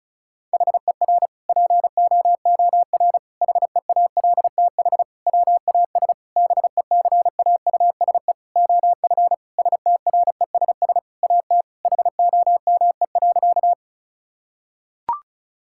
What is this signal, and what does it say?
Morse code, heard as a keyed tone, about 36 wpm, 700 Hz. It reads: HER POOR HEALTH WAS BECAUSE OF STRESS AT HOME.
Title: Sentences from Top 1000 Words - Review Part 6 - 35wpm